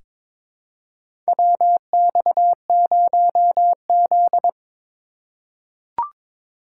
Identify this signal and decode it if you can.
Morse code, heard as a keyed tone, about 22 wpm, 700 Hz. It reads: WX0Z
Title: K1USN SST QSO Practice - 22wpm